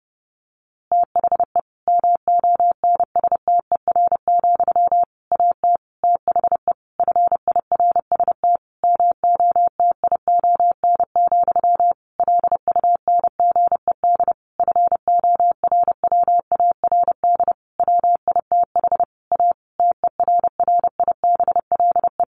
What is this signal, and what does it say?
Morse code, heard as a keyed tone, about 30 wpm, 700 Hz. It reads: THE MONSTER, AT THE FIRST MOTION, LUNGED FORWARD WITH A TERRIBLE
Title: A Sound Of Thunder 30wpm